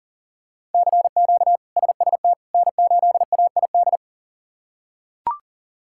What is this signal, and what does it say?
Morse code, heard as a keyed tone, about 40 wpm, 700 Hz. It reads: CQ SST N8AID
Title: K1USN SST QSO Practice - 40wpm